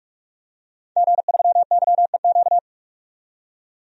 Morse code, heard as a keyed tone, about 47 wpm, 700 Hz. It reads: G3YEQ
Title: DXCC-100 Call Signs - 5 Characters - 45wpm